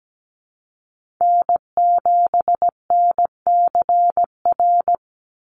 Morse code, heard as a keyed tone, about 17 wpm, 700 Hz. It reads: N7NCR
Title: USA Call Signs 17wpm